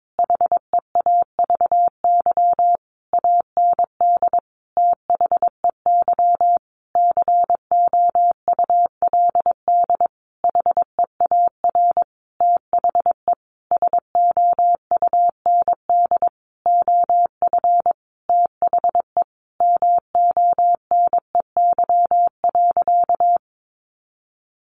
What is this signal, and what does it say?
Morse code, heard as a keyed tone, about 22 wpm, 700 Hz. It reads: HEAVY AND THEY COULD HEAR THE SOUND OF THE MONEY.
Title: Sentences from Top 1100 Words 22wpm